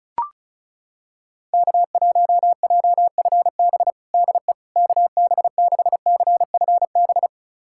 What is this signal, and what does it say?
Morse code, heard as a keyed tone, about 34 wpm, 700 Hz. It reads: K1JFB DE KB6CFB
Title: USA Callsign DE Callsign 35wpm